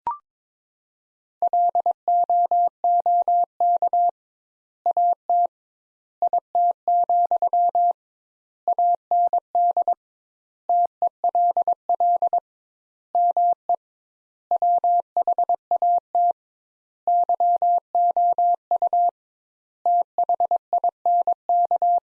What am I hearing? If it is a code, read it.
Morse code, heard as a keyed tone, about 21 wpm, 700 Hz. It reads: LOOK AT IT, AND TELL ME WHAT YOU THINK
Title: Sentences Top 200 Words - 22wpm